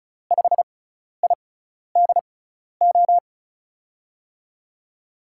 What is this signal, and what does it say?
Morse code, heard as a keyed tone, about 34 wpm, spaced as for 12 wpm, 700 Hz. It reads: 5IDO
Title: USA Call Signs - 6x Character Spacing - 35wpm